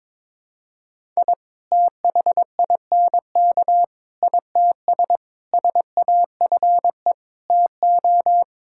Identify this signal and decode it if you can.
Morse code, heard as a keyed tone, about 22 wpm, 700 Hz. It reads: I THINK ITS SAFE TO
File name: practice / 22wpm / Sets of 7 Words - 22wpm